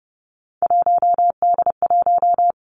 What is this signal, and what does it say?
Morse code, heard as a keyed tone, about 30 wpm, 700 Hz. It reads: 1D1
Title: Three Challenging Letter-Number Combinations 30wpm